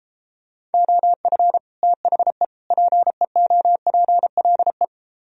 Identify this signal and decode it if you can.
Morse code, heard as a keyed tone, about 33 wpm, 700 Hz. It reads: OF THE PEOPLE